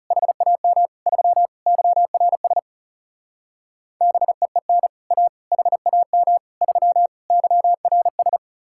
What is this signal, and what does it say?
Morse code, heard as a keyed tone, about 38 wpm, 700 Hz. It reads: HAM 3 YRS BEEN A HAM 3 YRS